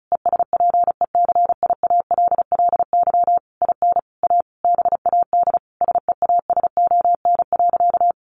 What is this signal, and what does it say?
Morse code, heard as a keyed tone, about 34 wpm, 700 Hz. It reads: ESPECIALLY IN A BAD SEASON.
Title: Sentences from Top 1400 Words 35wpm